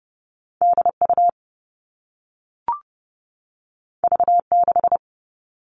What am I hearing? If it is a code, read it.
Morse code, heard as a keyed tone, about 30 wpm, 700 Hz. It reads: DU V6
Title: All Cut Numbers - 2-Digit 30wpm